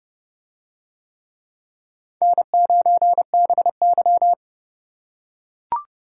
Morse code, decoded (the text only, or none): N9BY